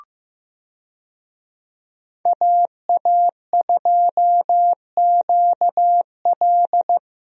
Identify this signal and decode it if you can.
Morse code, heard as a keyed tone, about 15 wpm, 700 Hz. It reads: AA2QL
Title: USA Call Signs - 5 Characters - 15wpm